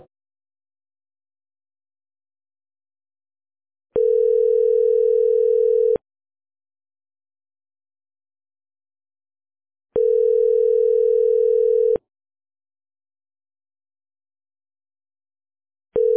Ringback tone (North America), 440 + 480 Hz.